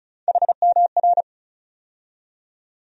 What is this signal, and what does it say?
Morse code, heard as a keyed tone, about 36 wpm, 700 Hz. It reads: HMR
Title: USA Call Sign Postfixes - 3 Letters 35wpm